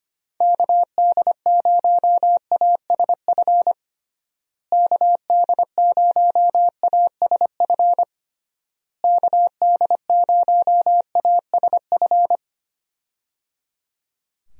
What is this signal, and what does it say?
Morse code, heard as a keyed tone, about 25 wpm, 700 Hz. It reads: KD0ASF KD0ASF KD0ASF
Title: USA Call Signs - 6 Characters - Repeated 3x - 25wpm